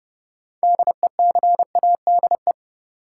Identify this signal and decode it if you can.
Morse code, heard as a keyed tone, about 32 wpm, 700 Hz. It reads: DECADE
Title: Sets of 5 Words - New Word Drill - 30wpm